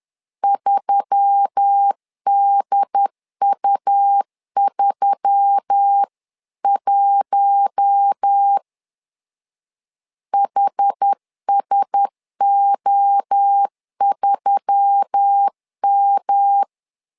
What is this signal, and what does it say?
Morse code, heard as a keyed tone, about 10 wpm, 800 Hz. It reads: 3DU31 HSO3M
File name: LEZIONI / 4 Q1S3 10wpm 2427